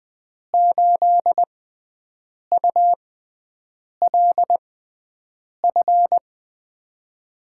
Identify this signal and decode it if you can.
Morse code, heard as a keyed tone, about 20 wpm, 700 Hz. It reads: 8 U L F
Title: USA Call Signs - 6x Character Spacing - 20wpm